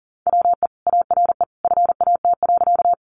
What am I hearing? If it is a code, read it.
Morse code, heard as a keyed tone, about 40 wpm, 700 Hz. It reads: WE ARE FAT.
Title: Sentences from Top 600 Words 40wpm